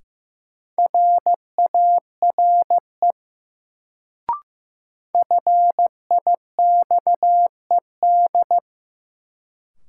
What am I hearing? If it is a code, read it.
Morse code, heard as a keyed tone, about 15 wpm, 700 Hz. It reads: RARE FIXED
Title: Sentences from Top 1100 Words - New Word Drill - 15wpm